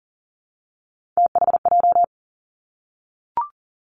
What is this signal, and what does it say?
Morse code, heard as a keyed tone, about 42 wpm, 700 Hz. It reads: THJ
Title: Three Challenging Letter-Number Combinations 40wpm